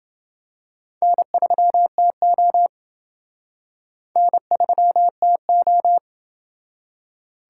Morse code, heard as a keyed tone, about 28 wpm, 700 Hz. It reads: N3TO N3TO